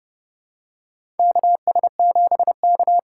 Morse code, heard as a keyed tone, about 30 wpm, 700 Hz. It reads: KS7K
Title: USA Call Signs - 4x Character Spacing - 30wpm